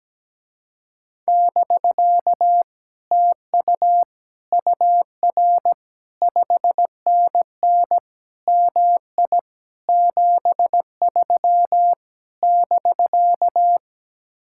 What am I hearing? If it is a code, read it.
Morse code, heard as a keyed tone, about 17 wpm, 700 Hz. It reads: <BK> TU UR 5NN MI 73 <BK>
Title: Speed-Racing Common QSO Phrases - 17wpm